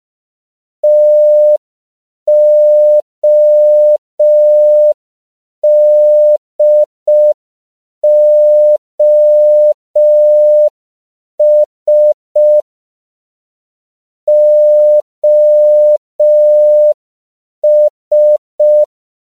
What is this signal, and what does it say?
Morse code, heard as a keyed tone, about 5 wpm, 600 Hz. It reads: TODOS OS
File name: CW / 5PPM